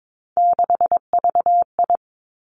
Morse code, heard as a keyed tone, about 22 wpm, 700 Hz. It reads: 6VI
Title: Three Challenging Letter-Number Combinations 22wpm